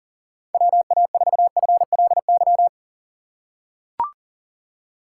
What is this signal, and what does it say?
Morse code, heard as a keyed tone, about 40 wpm, 700 Hz. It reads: WA4FLY
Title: USA Call Signs - 2x Character Spacing - 40wpm